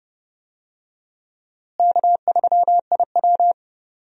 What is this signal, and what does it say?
Morse code, heard as a keyed tone, about 32 wpm, 700 Hz. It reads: K3IW
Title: USA Call Signs - 4 Characters - Repeated 3x - 30wpm